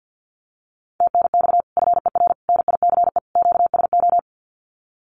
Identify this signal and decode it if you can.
Morse code, heard as a keyed tone, about 50 wpm, 700 Hz. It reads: TNX FER NICE QSO
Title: Common QSO Phrases Encore 50wpm